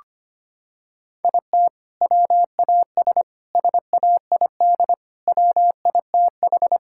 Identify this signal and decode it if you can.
Morse code, heard as a keyed tone, about 25 wpm, 700 Hz. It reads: IT WAS SAID WITH
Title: Sentences from Top 1600 Words 25wpm